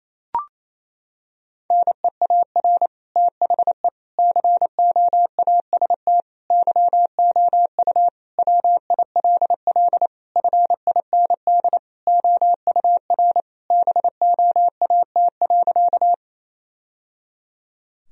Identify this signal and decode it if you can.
Morse code, heard as a keyed tone, about 27 wpm, 700 Hz. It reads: NEAR THE COAST YOU WILL FIND OUR BOAT.
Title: Sentences from Top 1000 Words - Review Part 4 - 28wpm